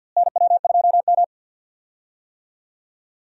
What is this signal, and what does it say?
Morse code, heard as a keyed tone, about 50 wpm, 700 Hz. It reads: NW2M
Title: USA Call Signs - 4 Characters - Repeated 2x - 50wpm